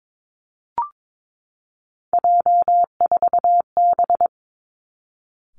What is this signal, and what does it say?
Morse code, heard as a keyed tone, about 22 wpm, 700 Hz. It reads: J4B
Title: Three Challenging Letter-Number Combinations 22wpm